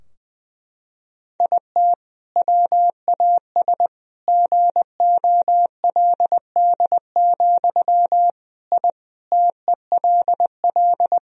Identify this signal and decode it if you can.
Morse code, heard as a keyed tone, about 20 wpm, 700 Hz. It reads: IT WAS GOLD, I TELL
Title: Sentences from Top 1000 Words - Review Part 3 - 20wpm